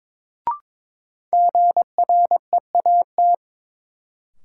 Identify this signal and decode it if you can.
Morse code, heard as a keyed tone, about 22 wpm, 700 Hz. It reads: GREAT
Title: Sentences from Top 200 Words - New Word Drill - 22wpm